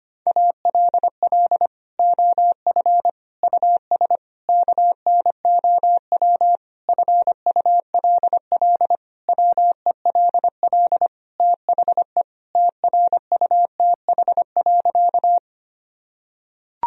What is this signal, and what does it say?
Morse code, heard as a keyed tone, about 25 wpm, 700 Hz. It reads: ALL OF US KNOW FULL WELL THE TRUTH.
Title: Sentences from Top 1000 Words - Review Part 3 - 25wpm